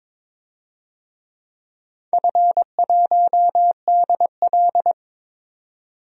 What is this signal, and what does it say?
Morse code, heard as a keyed tone, about 22 wpm, 700 Hz. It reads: F1DL
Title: DXCC-100 Call Signs - 4 Characters - 22wpm